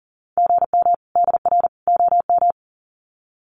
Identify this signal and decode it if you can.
Morse code, heard as a keyed tone, about 40 wpm, 700 Hz. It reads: GM DR OM
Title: Common QSO Phrases 40wpm